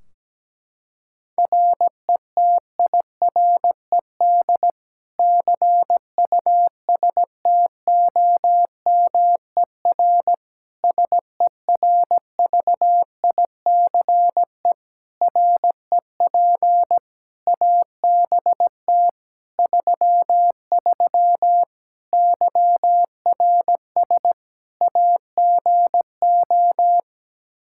Morse code, heard as a keyed tone, about 17 wpm, 700 Hz. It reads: RETIRED CUSTOMER SERVICE REP ABT 33 YRS AGO
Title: Practice QSO - Exchange 3 - More Info - 17wpm